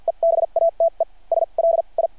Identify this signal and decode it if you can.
Morse code, heard as a keyed tone, about 52 wpm, spaced as for 35 wpm, 660 Hz. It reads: EBATES LI
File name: archived / CRSnet